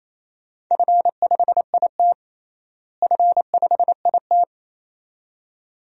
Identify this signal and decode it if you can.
Morse code, heard as a keyed tone, about 28 wpm, 700 Hz. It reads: F5IT F5IT
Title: DXCC-100 Call Signs - 4 Characters - Repeated 2x - 28wpm